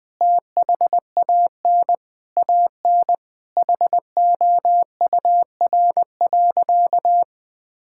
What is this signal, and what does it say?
Morse code, heard as a keyed tone, about 21 wpm, 700 Hz. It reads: THAN AN HOUR.
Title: Sentences with 3-Word Phrases - 20wpm